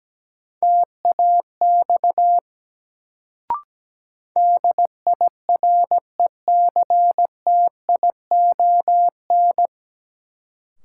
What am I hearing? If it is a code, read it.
Morse code, heard as a keyed tone, about 17 wpm, 700 Hz. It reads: TAX DIRECTION
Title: Sentences from Top 1000 Words - Review Part 5 - New Word Drill - 17wpm